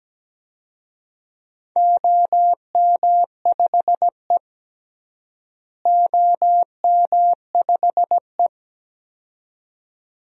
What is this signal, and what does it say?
Morse code, heard as a keyed tone, about 17 wpm, 700 Hz. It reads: OM5E OM5E
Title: DXCC-100 Call Signs - 4 Characters - Repeated 2x - 17wpm